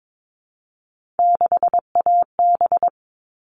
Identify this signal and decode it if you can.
Morse code, heard as a keyed tone, about 22 wpm, 700 Hz. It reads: 6AB
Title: All Cut Numbers - 3-Digit 22wpm